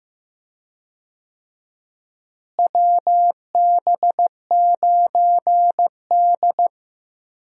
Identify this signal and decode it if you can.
Morse code, heard as a keyed tone, about 15 wpm, 700 Hz. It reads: WB9D
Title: USA Call Signs - 4 Characters - Repeated 2x - 15wpm